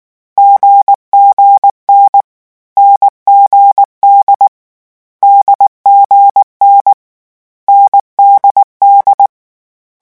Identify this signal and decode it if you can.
Morse code, heard as a keyed tone, about 19 wpm, 800 Hz. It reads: GGN NGD DGN NDD
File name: morse mp3 / lecon4-vitesse 18